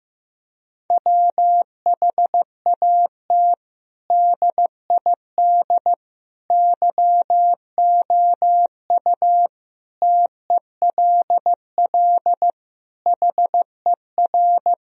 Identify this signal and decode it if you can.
Morse code, heard as a keyed tone, about 15 wpm, 700 Hz. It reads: WHAT DID YOU TELL HER
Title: Sentences from Top 1000 Words - Review Part 4 - 15wpm